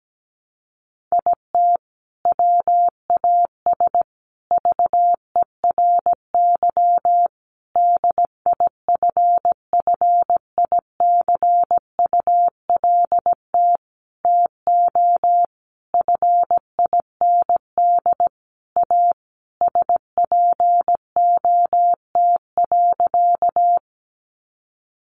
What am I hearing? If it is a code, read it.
Morse code, heard as a keyed tone, about 17 wpm, 700 Hz. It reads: IT WAS VERY DIFFICULT TO FIND A SPOT.
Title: Sentences from Top 1100 Words 17wpm